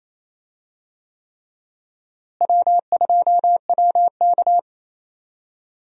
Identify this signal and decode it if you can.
Morse code, heard as a keyed tone, about 28 wpm, 700 Hz. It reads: W2WK
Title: USA Call Signs - 4 Characters - Repeated 3x - 28wpm